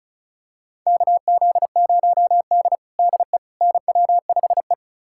Morse code, heard as a keyed tone, about 36 wpm, 700 Hz. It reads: KZ0D DE NW5E
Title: USA Callsign DE Callsign - 4 Characters - 35wpm